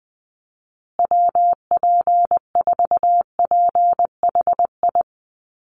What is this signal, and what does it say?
Morse code, heard as a keyed tone, about 20 wpm, 700 Hz. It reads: WP4PHI
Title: USA Call Signs Repeated 3x 20wpm